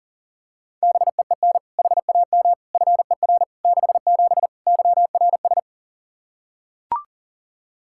Morse code, heard as a keyed tone, about 40 wpm, 700 Hz. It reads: BEEN HAM FER 67 YRS
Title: Practice QSO - Exchange 3 - More Info - 40wpm